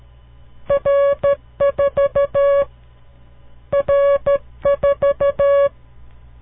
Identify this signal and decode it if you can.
Morse code, heard as a keyed tone, about 13 wpm, 560 Hz. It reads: R4 R4
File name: CW Learning Programs / farnsworth / unit 5